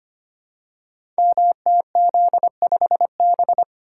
Morse code, heard as a keyed tone, about 25 wpm, 700 Hz. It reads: MTZ5B
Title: Code Groups - 5 Letter:Numbers - 25wpm